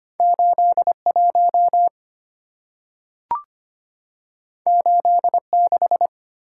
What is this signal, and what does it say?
Morse code, heard as a keyed tone, about 25 wpm, 690 Hz. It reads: 81 86
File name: Two Letter-Number Combinations - 25wpm